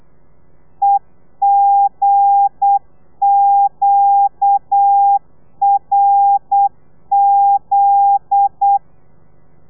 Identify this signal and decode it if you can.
Morse code, heard as a keyed tone, about 8 wpm, 800 Hz. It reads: EGQRZ
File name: telegrafia / i2tfj / Lez9 35